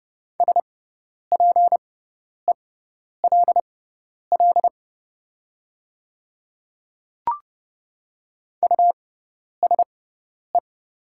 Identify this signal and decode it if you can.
Morse code, heard as a keyed tone, about 30 wpm, spaced as for 10 wpm, 700 Hz. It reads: SPELL USE
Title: 6x Character Spacing - Top 300 Words - 30wpm